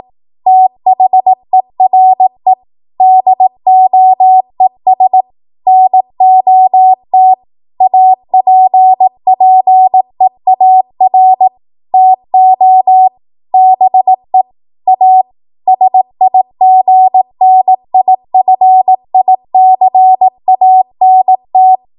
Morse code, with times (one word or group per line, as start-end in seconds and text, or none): 0.5-2.5 THERE
3.0-5.2 DOES
5.7-7.3 NOT
7.8-11.5 APPEAR
11.9-13.1 TO
13.5-14.4 BE
14.9-15.2 A
15.7-21.8 SIGNIFICANT